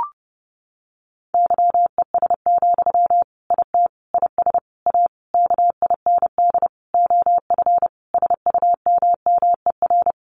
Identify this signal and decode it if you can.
Morse code, heard as a keyed tone, about 28 wpm, 700 Hz. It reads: YES, IT IS A KIND OF SUMMER
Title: Sentences from Top 1400 Words 30wpm